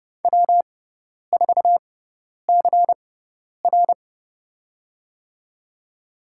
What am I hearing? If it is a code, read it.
Morse code, heard as a keyed tone, about 31 wpm, spaced as for 10 wpm, 700 Hz. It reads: W4CR
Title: USA Call Signs - 6x Character Spacing - 30wpm